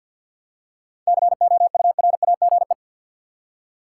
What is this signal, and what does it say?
Morse code, heard as a keyed tone, about 53 wpm, 700 Hz. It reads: COURAGE
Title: Seven Letter Words - 50wpm